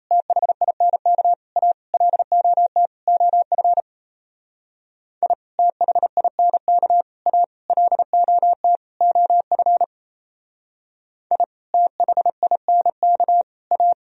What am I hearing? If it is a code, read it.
Morse code, heard as a keyed tone, about 35 wpm, 700 Hz. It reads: THINK A LOT OF I THINK A LOT OF I THINK A